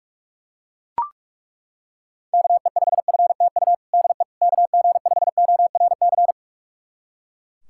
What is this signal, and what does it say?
Morse code, heard as a keyed tone, about 42 wpm, 690 Hz. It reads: KE5FTU DE KG5ORC
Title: USA Callsign DE Callsign 45wpm